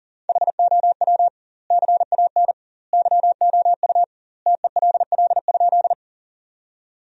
Morse code, heard as a keyed tone, about 40 wpm, 700 Hz. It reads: HOW CAN YOU TELL?
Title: 2x Word-Spacing - Sentences from Top 200 Words - 40wpm